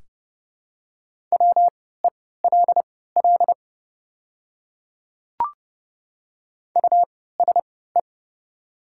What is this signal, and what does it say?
Morse code, heard as a keyed tone, about 31 wpm, spaced as for 17 wpm, 690 Hz. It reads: WELL USE
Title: 3x Character Spacing - Top 200 Words - 30wpm